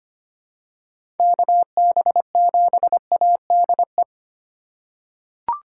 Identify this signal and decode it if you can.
Morse code, heard as a keyed tone, about 25 wpm, 700 Hz. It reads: KB7ADE
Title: USA Call Signs - 4x Character Spacing - 25wpm